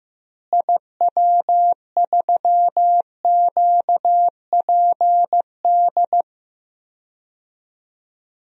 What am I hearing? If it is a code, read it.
Morse code, heard as a keyed tone, about 15 wpm, 700 Hz. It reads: IW3QPD
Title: DXCC-100 Call Signs - 6 Characters - Repeated 3x - 15wpm